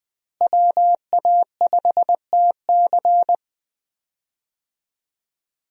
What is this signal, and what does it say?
Morse code, heard as a keyed tone, about 20 wpm, 700 Hz. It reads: WA5TC
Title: USA Call Signs - 5 Characters - Repeated 3x - 20wpm